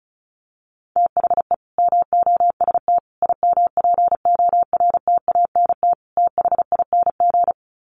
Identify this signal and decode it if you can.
Morse code, heard as a keyed tone, about 34 wpm, 700 Hz. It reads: THE MOST IMPORTANT THING
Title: Sets of 4 Words 35wpm